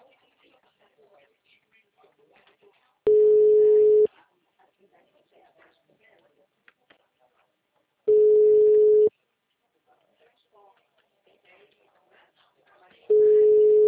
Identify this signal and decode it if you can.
Ringback tone (Europe, 425 Hz), with speech.
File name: out- -2011- - - .4678